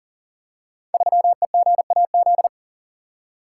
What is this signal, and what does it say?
Morse code, heard as a keyed tone, about 40 wpm, 700 Hz. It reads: 3EGAZ
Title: Code Groups - 5 Letter:Numbers - 40wpm